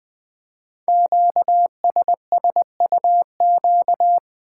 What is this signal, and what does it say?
Morse code, heard as a keyed tone, about 20 wpm, 700 Hz. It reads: QSSUQ
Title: Code Groups - 5 Letters - 20wpm